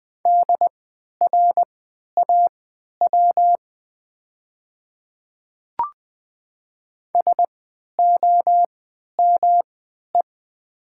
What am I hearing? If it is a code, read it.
Morse code, heard as a keyed tone, about 20 wpm, spaced as for 11 wpm, 700 Hz. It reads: DRAW SOME
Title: 3x Character Spacing - Top 300 Words - 20wpm